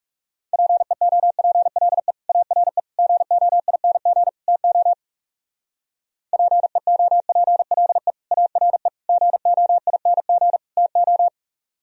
Morse code, heard as a keyed tone, about 34 wpm, 700 Hz. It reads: PEOPLE ARE GOING TO PEOPLE ARE GOING TO
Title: Speed-Racing - Sets of 4 Words 30wpm